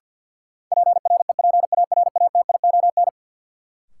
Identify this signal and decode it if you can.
Morse code, heard as a keyed tone, about 49 wpm, 700 Hz. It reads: PREPARATION
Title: Sentences from Top 1400 Words - New Word Drill - 50wpm